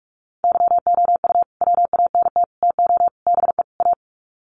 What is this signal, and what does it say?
Morse code, heard as a keyed tone, about 44 wpm, 700 Hz. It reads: YOU WANT TO BE A
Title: Sets of 5 Words Encore2 45wpm